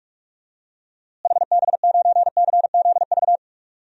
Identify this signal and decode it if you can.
Morse code, heard as a keyed tone, about 46 wpm, 700 Hz. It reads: HB9CZV
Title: DXCC-100 Call Signs - 6 Characters - Repeated 3x - 45wpm